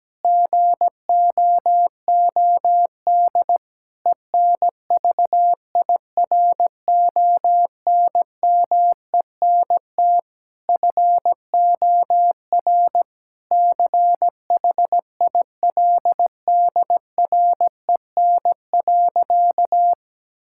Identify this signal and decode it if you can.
Morse code, heard as a keyed tone, about 17 wpm, 700 Hz. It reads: GOOD ENVIRONMENT FOR CHILDREN.
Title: Sentences from Top 1000 Words - Review Part 5 - 17wpm